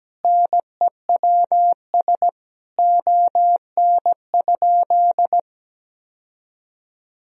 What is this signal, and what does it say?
Morse code, heard as a keyed tone, about 17 wpm, 700 Hz. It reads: NEWS ON?
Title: Sentences from Top 1000 Words - Review Part 3 - 17wpm